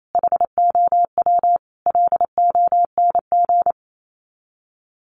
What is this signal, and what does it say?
Morse code, heard as a keyed tone, about 28 wpm, 700 Hz. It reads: HOW LONG
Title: Sets of 5 Words Encore2 28wpm